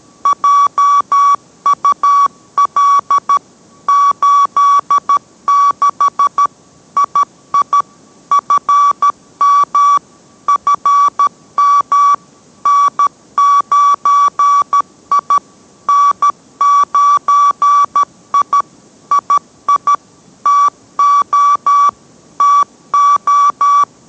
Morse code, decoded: JUL 86 II FM FM N9I N9I II TO TO

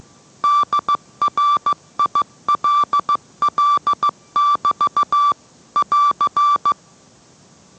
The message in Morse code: DRILL <BT> <AR>

